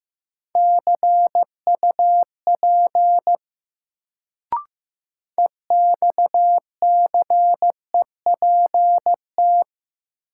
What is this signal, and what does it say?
Morse code, heard as a keyed tone, about 15 wpm, 700 Hz. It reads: CUP EXCEPT
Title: Sentences from Top 1600 Words - New Word Drill - 15wpm